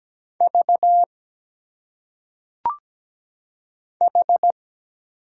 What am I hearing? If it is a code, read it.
Morse code, heard as a keyed tone, about 17 wpm, 700 Hz. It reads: V H
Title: Single Letters - Phonetic Pronunciation 17wpm